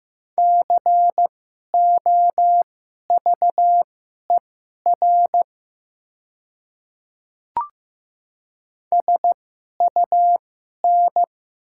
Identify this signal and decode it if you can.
Morse code, heard as a keyed tone, about 15 wpm, spaced as for 11 wpm, 700 Hz. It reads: COVER SUN
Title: 2x Character Spacing - Top 300 Words - 15wpm